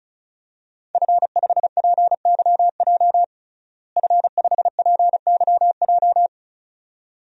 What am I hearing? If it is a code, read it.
Morse code, heard as a keyed tone, about 36 wpm, 700 Hz. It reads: F5PYJ F5PYJ
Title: DXCC-100 Call Signs - 5 Characters - Repeated 2x - 35wpm